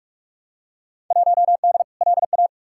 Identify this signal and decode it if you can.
Morse code, heard as a keyed tone, about 44 wpm, 700 Hz. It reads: 1D LA
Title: Field Day Exchanges - 45wpm